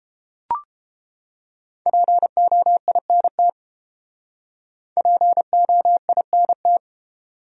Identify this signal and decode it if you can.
Morse code, heard as a keyed tone, about 30 wpm, 700 Hz. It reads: POINT POINT